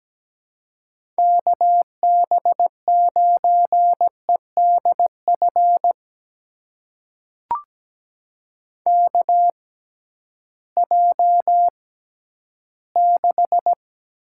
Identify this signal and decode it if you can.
Morse code, heard as a keyed tone, about 17 wpm, spaced as for 11 wpm, 700 Hz. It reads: KB9EDF KJ6
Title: USA Call Signs - 6 Characters - 6x Character Spacing - 17wpm